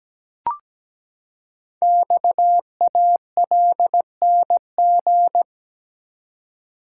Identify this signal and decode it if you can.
Morse code, heard as a keyed tone, about 17 wpm, 700 Hz. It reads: XALNG
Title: Code Groups - 5 Letters - 17wpm